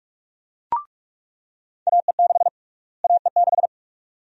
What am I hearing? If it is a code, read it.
Morse code, heard as a keyed tone, about 46 wpm, 700 Hz. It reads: AE6 AE6
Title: USA Call Sign Prefixes - Repeated 2x - 45wpm